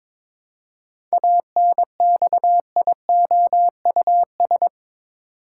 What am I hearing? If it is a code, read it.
Morse code, heard as a keyed tone, about 22 wpm, 700 Hz. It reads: ANXIOUS
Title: Seven Letter Words - 22wpm